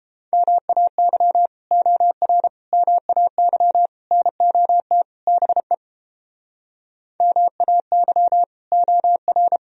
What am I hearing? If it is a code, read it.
Morse code, heard as a keyed tone, about 29 wpm, 700 Hz. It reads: MAY OR MAY NOT BE MAY OR